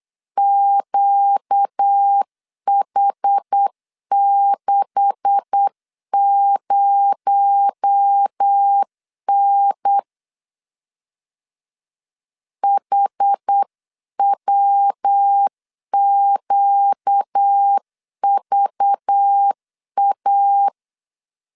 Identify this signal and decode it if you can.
Morse code, heard as a keyed tone, about 8 wpm, 800 Hz. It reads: QH60N HWQVA
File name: LEZIONI / 9 286 8wpm 3425